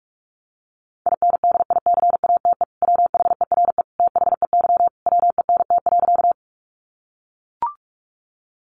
Morse code, heard as a keyed tone, about 44 wpm, 700 Hz. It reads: INDICATE WHERE THEY WENT.
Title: Sentences from Top 1300 Words 45wpm